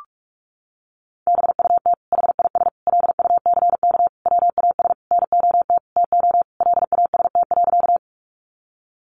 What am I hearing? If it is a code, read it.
Morse code, heard as a keyed tone, about 44 wpm, 700 Hz. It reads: BUT HIS LUCK WAS NOT TO LAST.
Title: Sentences from Top 1500 Words 45wpm